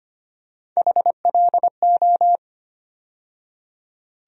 Morse code, heard as a keyed tone, about 25 wpm, 700 Hz. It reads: HLO
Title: USA Call Sign Postfixes - 3 Letters 25wpm